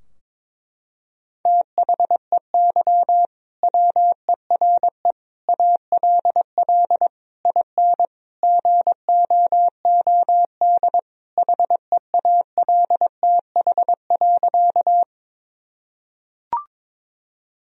Morse code, heard as a keyed tone, about 22 wpm, 700 Hz. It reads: THEY WERE ALL IN GOOD HEALTH.
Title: Sentences from Top 1000 Words - Review Part 3 - 22wpm